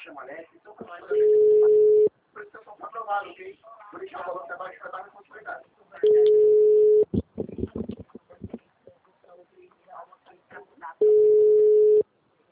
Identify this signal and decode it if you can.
Ringback tone (Europe, 425 Hz), with speech.